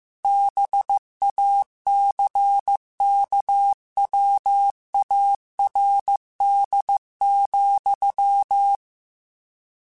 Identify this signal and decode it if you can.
Morse code, heard as a keyed tone, about 15 wpm, 800 Hz. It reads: BACKWARD,